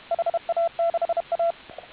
Morse code, heard as a keyed tone, about 31 wpm, 690 Hz, with radio noise. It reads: HA6A